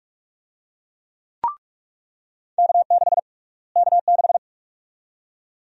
Morse code, heard as a keyed tone, about 46 wpm, 700 Hz. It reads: K6 K6
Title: USA Call Sign Prefixes - Repeated 2x - 45wpm